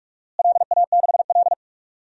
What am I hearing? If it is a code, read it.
Morse code, heard as a keyed tone, about 46 wpm, 700 Hz. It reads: LA6L
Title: DXCC-100 Call Signs - 4 Characters - Repeated 3x - 45wpm